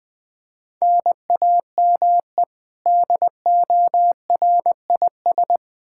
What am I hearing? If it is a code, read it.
Morse code, heard as a keyed tone, about 20 wpm, 700 Hz. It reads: NAME DORIS
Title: Practice QSO - Exchange 1 - Beginning - 20wpm